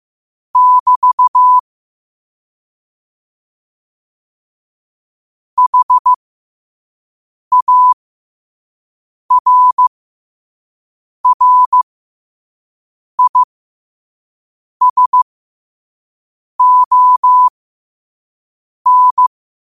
Morse code, heard as a keyed tone, about 15 wpm, spaced as for 5 wpm, 1000 Hz. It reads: <BT> HARRISON
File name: qotdfiles / 5wpm